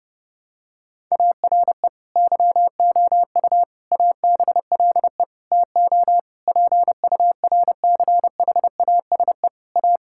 Morse code, heard as a keyed tone, about 30 wpm, 700 Hz. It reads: ARE YOU ABLE TO PURCHASE A
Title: Sentences from Top 1000 Words - Review Part 5 - 30wpm